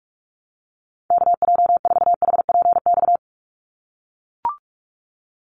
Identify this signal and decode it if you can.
Morse code, heard as a keyed tone, about 44 wpm, 700 Hz. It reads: KJ4HPX
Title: USA Call Signs 45wpm